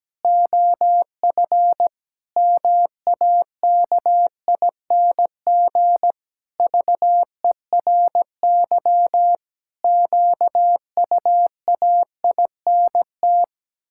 Morse code, heard as a keyed tone, about 17 wpm, 700 Hz. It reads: OF MAKING VERY QUAINT
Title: Pangram Challenge v2 - 17wpm